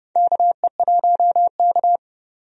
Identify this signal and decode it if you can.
Morse code, heard as a keyed tone, about 31 wpm, 700 Hz. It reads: KE1K
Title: USA Call Signs - 4 Characters - 30wpm